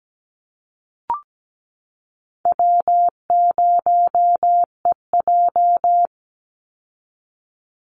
Morse code, heard as a keyed tone, about 17 wpm, 700 Hz. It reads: W0EJ
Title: Easy Call Signs Repeated 2x - 17wpm